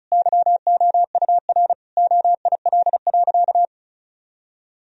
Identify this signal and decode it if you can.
Morse code, heard as a keyed tone, about 34 wpm, 700 Hz. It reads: YOUR OIL.
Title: Sentences from Top 1000 Words - Review Part 4 - 35wpm